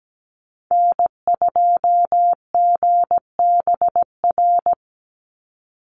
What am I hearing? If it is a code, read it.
Morse code, heard as a keyed tone, about 17 wpm, 700 Hz. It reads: N2GBR
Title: NA SOTA Chaser Callsigns 17wpm